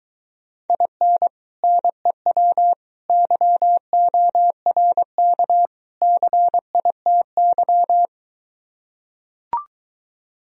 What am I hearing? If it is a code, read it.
Morse code, heard as a keyed tone, about 23 wpm, 700 Hz. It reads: IN NEW YORK CITY